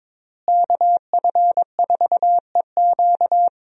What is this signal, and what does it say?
Morse code, heard as a keyed tone, about 22 wpm, 700 Hz. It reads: KF4EQ
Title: USA Call Signs - 5 Characters - Repeated 2x - 22wpm